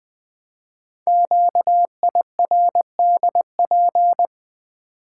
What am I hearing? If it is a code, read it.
Morse code, heard as a keyed tone, about 20 wpm, 700 Hz. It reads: QIRDP
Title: Code Groups - 5 Letter:Numbers - 20wpm